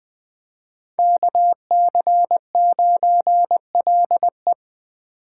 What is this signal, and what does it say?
Morse code, heard as a keyed tone, about 20 wpm, 700 Hz. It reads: KC9LE
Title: USA Call Signs - 6x Character Spacing - 20wpm